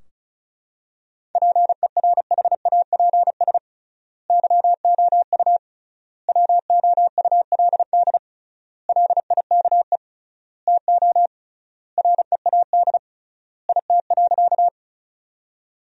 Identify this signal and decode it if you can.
Morse code, heard as a keyed tone, about 36 wpm, 700 Hz. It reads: PERHAPS YOU WOULD LIKE TO READ IT.
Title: 3x Word-Spacing - Sentences from Top 500 Words - 35wpm